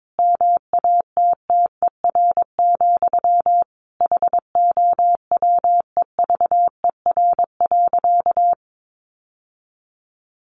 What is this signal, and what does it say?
Morse code, heard as a keyed tone, about 22 wpm, 700 Hz. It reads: MATTER, HOWEVER.
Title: Sentences from Top 700 Words 22wpm